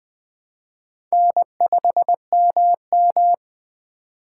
Morse code, heard as a keyed tone, about 20 wpm, 700 Hz. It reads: N5MM
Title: Speed Racing - USA 1x2 Call Signs 17wpm